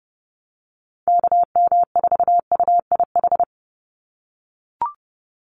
Morse code, heard as a keyed tone, about 30 wpm, 700 Hz. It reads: KM4UIH
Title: USA Call Signs 30wpm